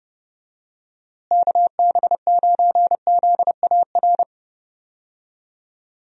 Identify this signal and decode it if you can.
Morse code, heard as a keyed tone, about 32 wpm, 700 Hz. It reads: KB9ZAR
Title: USA Call Signs - 6 Characters - Repeated 3x - 30wpm